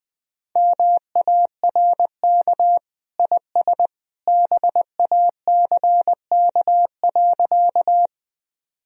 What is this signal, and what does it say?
Morse code, heard as a keyed tone, about 20 wpm, 700 Hz. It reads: MARK IS BACK.
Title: Sentences from Top 1000 Words - Review Part 1 - 20wpm